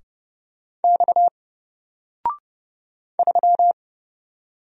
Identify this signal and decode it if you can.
Morse code, heard as a keyed tone, about 30 wpm, 700 Hz. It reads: X 3
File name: Single-Letter-Number-30wpm